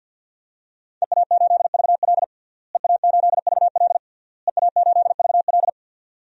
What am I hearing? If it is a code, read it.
Morse code, heard as a keyed tone, about 52 wpm, 700 Hz. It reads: EA8VL EA8VL EA8VL
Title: DXCC-100 Call Signs - 5 Characters - Repeated 3x - 50wpm